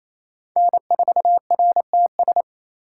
Morse code, heard as a keyed tone, about 28 wpm, 700 Hz. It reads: N4RTS
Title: USA Call Signs - 5 Characters - 4x Character Spacing - 28wpm